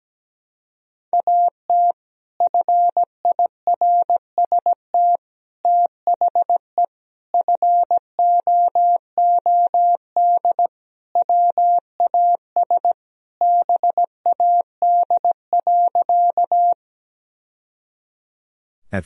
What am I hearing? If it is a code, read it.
Morse code, heard as a keyed tone, about 17 wpm, 700 Hz. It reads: AT FIRST THE FOOD WAS BAD.
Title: Sentences from Top 1000 Words - Review Part 1 - 17wpm